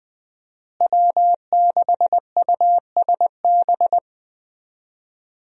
Morse code, heard as a keyed tone, about 20 wpm, 700 Hz. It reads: W6USB
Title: USA Call Signs - 5 Characters - Repeated 2x - 20wpm